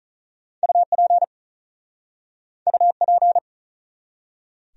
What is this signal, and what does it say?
Morse code, heard as a keyed tone, about 46 wpm, 700 Hz. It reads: UP UP